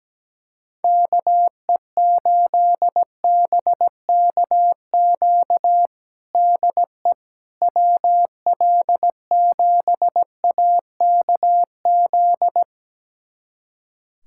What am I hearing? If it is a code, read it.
Morse code, heard as a keyed tone, about 17 wpm, 700 Hz. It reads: KE8BKQ DE WL7AKZ
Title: USA Callsign DE Callsign - 6 Characters - 17wpm